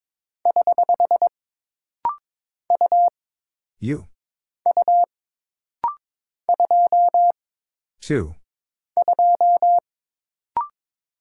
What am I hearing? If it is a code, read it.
Morse code, heard as a keyed tone, about 22 wpm, 700 Hz. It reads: <HH> U U 2 2
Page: This rapid-fire timing is geared to push your ICR (Instant Character Recognition) skills.